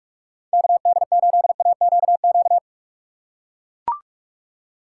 Morse code, heard as a keyed tone, about 42 wpm, 700 Hz. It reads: KD8AQQ
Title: USA Call Signs - 6 Characters - 4x Character Spacing - 45wpm